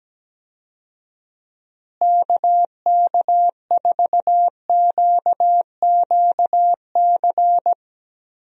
Morse code, heard as a keyed tone, about 17 wpm, 700 Hz. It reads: KK4QQC
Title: USA Call Signs - 6 Characters - Repeated 3x - 17wpm